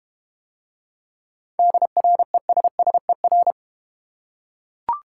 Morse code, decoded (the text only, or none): DRESSER